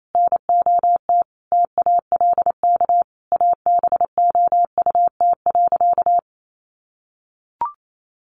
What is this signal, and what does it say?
Morse code, heard as a keyed tone, about 28 wpm, 700 Hz. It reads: NOT TALK ABOUT.
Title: Sentences from Top 800 Words 28wpm